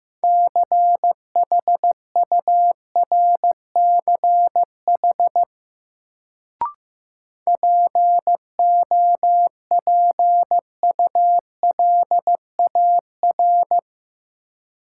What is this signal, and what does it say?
Morse code, heard as a keyed tone, about 15 wpm, 700 Hz. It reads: CHURCH POPULAR
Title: Sets of 5 Words - New Word Drill - 15wpm